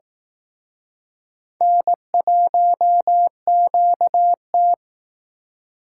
Morse code, heard as a keyed tone, about 18 wpm, 700 Hz. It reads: N1QT
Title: Speed Racing - USA 1x2 Call Signs 15wpm